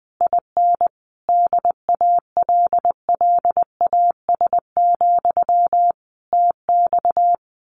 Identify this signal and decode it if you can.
Morse code, heard as a keyed tone, about 20 wpm, 700 Hz. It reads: IN DALLAS, TX
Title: Common QSO Phrases Encore 20wpm